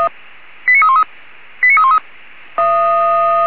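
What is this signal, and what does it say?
Five-tone selective call (ZVEI), heard as quick runs of five tones, tones of about 70 ms. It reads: 89313 89313
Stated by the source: ZVEI five-tone selective call received on an SDR.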